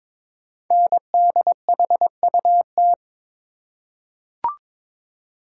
Morse code, heard as a keyed tone, about 22 wpm, 700 Hz. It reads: NDHUT
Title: Code Groups - 5 Letters - 22wpm